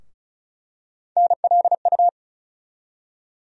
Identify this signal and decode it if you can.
Morse code, heard as a keyed tone, about 36 wpm, 700 Hz. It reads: NLU
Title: USA Call Sign Postfixes - 3 Letters 35wpm